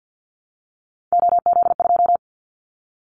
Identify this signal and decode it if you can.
Morse code, heard as a keyed tone, about 50 wpm, 700 Hz. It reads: OZ2
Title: Callsign Prefixes for Top 100 DXCC - 50wpm